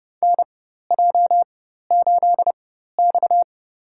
Morse code, heard as a keyed tone, about 30 wpm, spaced as for 14 wpm, 700 Hz. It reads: NJ8X
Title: USA Call Signs - 4 Characters - 4x Character Spacing - 30wpm